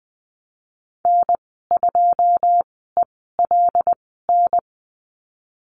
Morse code, heard as a keyed tone, about 19 wpm, spaced as for 14 wpm, 700 Hz. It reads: N2ELN
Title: 088 Lesson 15g - Number 5 - Call Signs 20wpm